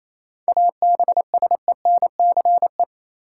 Morse code, heard as a keyed tone, about 28 wpm, 700 Hz. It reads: ABSENCE
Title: Sets of 4 Words - New Word Drill - 28wpm